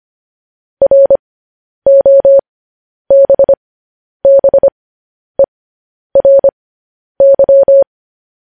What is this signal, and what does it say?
Morse code, heard as a keyed tone, about 25 wpm, spaced as for 10 wpm, 550 Hz. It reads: ROBBERY
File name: podcast / News headlines-1x-25x10